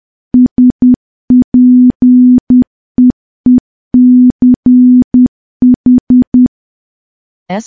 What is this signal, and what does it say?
Morse code, heard as a keyed tone, about 10 wpm, 250 Hz. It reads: SPEECH
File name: morse / morse-training-10WPM